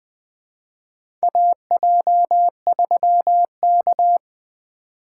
Morse code, heard as a keyed tone, about 20 wpm, 700 Hz. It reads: AJ3K
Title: USA Call Signs - 4 Characters - 3x Character Spacing - 20wpm